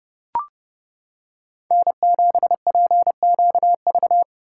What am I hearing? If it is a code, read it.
Morse code, heard as a keyed tone, about 30 wpm, 700 Hz. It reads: N7PQV